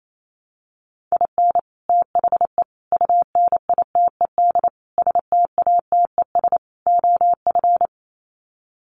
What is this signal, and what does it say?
Morse code, heard as a keyed tone, about 28 wpm, 700 Hz. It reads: IN THE UNITED STATES OF
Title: Sets of 5 Words Encore2 28wpm